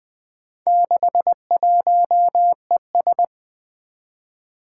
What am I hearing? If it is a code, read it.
Morse code, heard as a keyed tone, about 20 wpm, 700 Hz. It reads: 61ES
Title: Four Challenging Letter-Number Combinations 20wpm